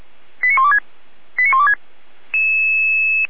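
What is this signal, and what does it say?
Five-tone selective call (ZVEI), heard as quick runs of five tones, tones of about 70 ms. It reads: 89126 89126
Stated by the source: ZVEI five-tone selective call received on an SDR.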